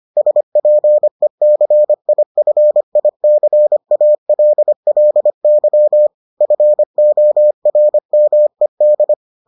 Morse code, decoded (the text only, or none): SPECIFICALLY FORMED